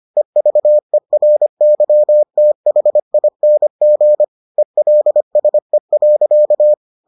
Morse code, heard as a keyed tone, about 25 wpm, 600 Hz. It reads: EVERYTHING ELSE.